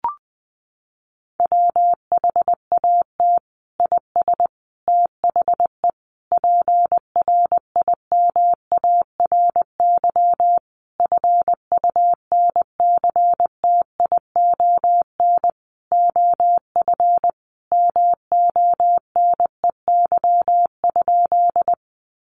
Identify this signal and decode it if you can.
Morse code, heard as a keyed tone, about 20 wpm, 700 Hz. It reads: WHAT IS THE PRIMARY FUNCTION OF MONEY?
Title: Sentences from Top 1100 Words 20wpm